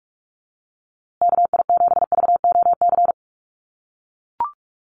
Morse code, heard as a keyed tone, about 44 wpm, 700 Hz. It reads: KI7VOC
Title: USA Call Signs Repeated 2x 45wpm